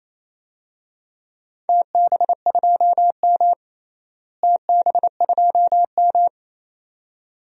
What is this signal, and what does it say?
Morse code, heard as a keyed tone, about 28 wpm, 700 Hz. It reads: TB2M TB2M
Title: DXCC-100 Call Signs - 4 Characters - Repeated 2x - 28wpm